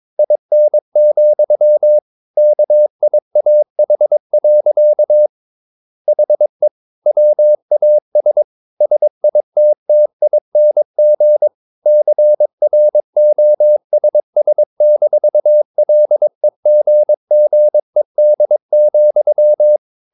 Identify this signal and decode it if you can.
Morse code, heard as a keyed tone, about 22 wpm, 600 Hz. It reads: IN, KIAH. HE WAS SITTING CROSS-LEGGED,
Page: Romanen Never af Ken Follet på engelsk.